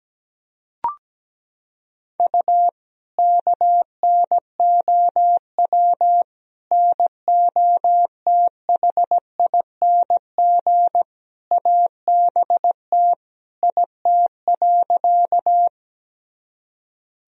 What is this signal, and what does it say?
Morse code, heard as a keyed tone, about 17 wpm, 700 Hz. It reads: U KNOW NOTHING ABT IT.
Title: Sentences from Top 500 Words and Abbreviations 17wpm